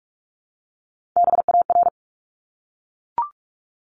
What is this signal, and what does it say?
Morse code, heard as a keyed tone, about 44 wpm, 700 Hz. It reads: BAR
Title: Top 1751-2000 Words 45wpm